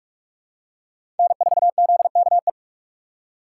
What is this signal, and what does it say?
Morse code, heard as a keyed tone, about 48 wpm, 700 Hz. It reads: N4ZKE
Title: USA Call Signs - 5 Characters - 4x Character Spacing - 45wpm